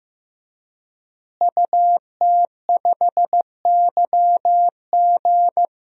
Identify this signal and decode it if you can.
Morse code, heard as a keyed tone, about 15 wpm, 700 Hz. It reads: UT5YG
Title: DXCC-100 Call Signs - 5 Characters - Repeated 3x - 15wpm